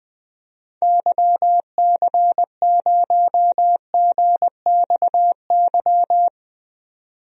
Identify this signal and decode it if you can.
Morse code, heard as a keyed tone, about 20 wpm, 700 Hz. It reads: YC0GXY
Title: DXCC-100 Call Signs - 6 Characters - 20wpm